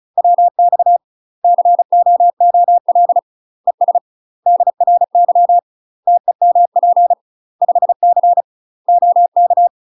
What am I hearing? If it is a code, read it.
Morse code, heard as a keyed tone, about 36 wpm, 700 Hz. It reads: WX COOL ES DRY TEMP 5C OK